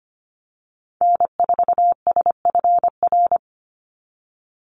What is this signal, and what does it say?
Morse code, heard as a keyed tone, about 25 wpm, 700 Hz. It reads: N4SFR
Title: NA SOTA Activator Callsigns 25wpm